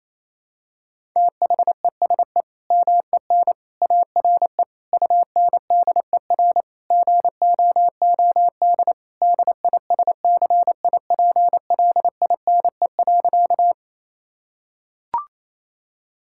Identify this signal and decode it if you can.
Morse code, heard as a keyed tone, about 28 wpm, 700 Hz. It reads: THESE MEN ARE UNDER GOOD DISCIPLINE.
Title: Sentences from Top 1000 Words - Review Part 6 - 28wpm